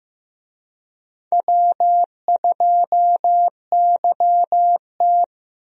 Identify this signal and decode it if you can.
Morse code, heard as a keyed tone, about 15 wpm, 700 Hz. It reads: W2YT
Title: USA Call Signs - 4 Characters - Repeated 3x - 15wpm